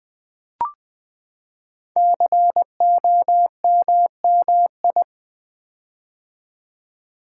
Morse code, heard as a keyed tone, about 20 wpm, 700 Hz. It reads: COMMI
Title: Build A Word - 5 Letters At A Time - 10-Letter Words - 20wpm